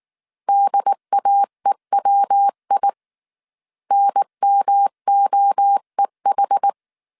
Morse code, heard as a keyed tone, about 19 wpm, 800 Hz. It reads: DAEWI NMOEH